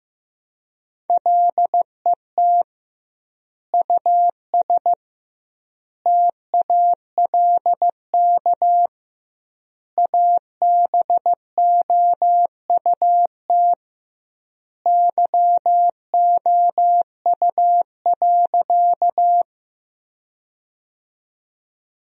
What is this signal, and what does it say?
Morse code, heard as a keyed tone, about 16 wpm, 700 Hz. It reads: LET US TALK ABOUT YOU.
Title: 2x Word-Spacing - Sentences from Top 400 Words - 15wpm